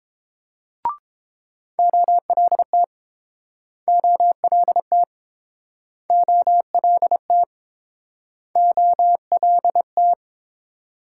Morse code, heard as a keyed tone, about 30 wpm, 700 Hz. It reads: OLT OLT OLT OLT